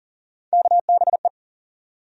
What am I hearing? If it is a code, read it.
Morse code, heard as a keyed tone, about 38 wpm, 700 Hz. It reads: KBE
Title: USA Call Sign Postfixes - 3 Letters 40wpm